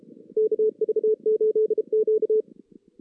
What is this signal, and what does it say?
Morse code, heard as a keyed tone, about 32 wpm, 430 Hz. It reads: KV8Q
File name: 2016 JAN CW NAQP